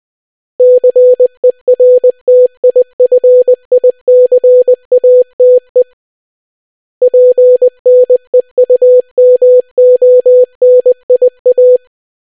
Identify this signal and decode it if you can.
Morse code, heard as a keyed tone, about 20 wpm, 500 Hz. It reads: CERTIFICATE PNEUMONIA